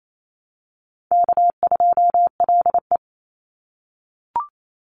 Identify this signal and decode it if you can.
Morse code, heard as a keyed tone, about 28 wpm, 700 Hz. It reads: K2LE
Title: Easy Call Signs - 28wpm